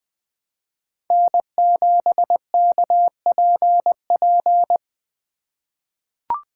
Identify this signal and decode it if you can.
Morse code, heard as a keyed tone, about 20 wpm, 700 Hz. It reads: N7KPP
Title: USA Call Signs - 4x Character Spacing - 20wpm